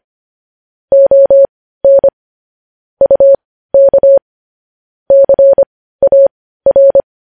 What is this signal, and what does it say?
Morse code, heard as a keyed tone, about 25 wpm, spaced as for 15 wpm, 550 Hz. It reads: ON UK CAR
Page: news-headlines-in-morse-code-at-15-wpm